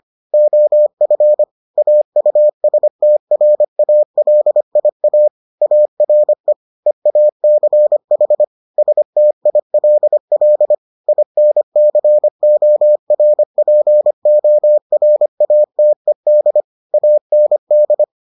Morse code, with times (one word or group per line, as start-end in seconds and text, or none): 0.3-1.4 OF
1.8-5.3 AUSTRALIA
5.6-6.5 ARE
6.9-8.4 EACH
8.8-10.7 STILL
11.1-16.6 INCORPORATED
16.9-18.0 AND